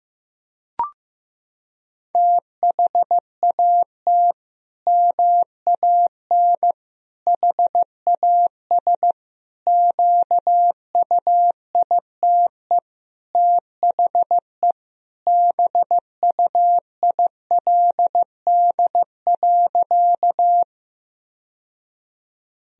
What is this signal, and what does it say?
Morse code, heard as a keyed tone, about 16 wpm, 700 Hz. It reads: THAT MAN HAS QUITE THE BUILD.
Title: Sentences from Top 1000 Words - Review Part 1 - 15wpm